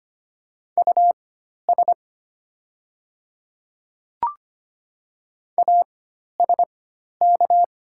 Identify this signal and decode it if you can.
Morse code, heard as a keyed tone, about 25 wpm, spaced as for 12 wpm, 700 Hz. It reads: US ASK
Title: 4x Character Spacing - Top 300 Words - 25wpm